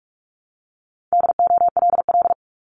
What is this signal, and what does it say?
Morse code, heard as a keyed tone, about 46 wpm, 700 Hz. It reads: DOLL
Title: Multi-letter Phonemes 45wpm